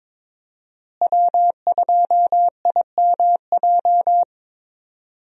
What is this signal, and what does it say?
Morse code, heard as a keyed tone, about 22 wpm, 700 Hz. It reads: W2IMJ
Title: USA Call Signs - 5 Characters - Repeated 3x - 22wpm